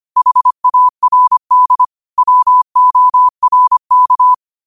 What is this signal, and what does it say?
Morse code, heard as a keyed tone, about 25 wpm, 1000 Hz. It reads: HARD WORK
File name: qotdfiles / 25wpm